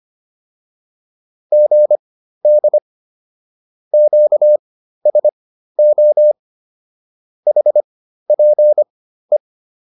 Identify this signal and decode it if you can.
Morse code, heard as a keyed tone, about 25 wpm, spaced as for 13 wpm, 600 Hz. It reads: GD QSO HPE